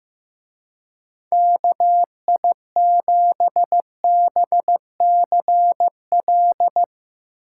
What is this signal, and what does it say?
Morse code, heard as a keyed tone, about 16 wpm, 700 Hz. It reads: KI7BCL
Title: Speed Racing - USA 2x3 Call Signs 15wpm